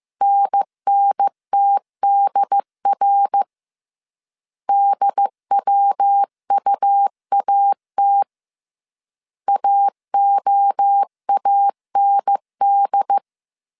Morse code, heard as a keyed tone, about 15 wpm, 800 Hz. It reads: NNTDR DWUAT AOAND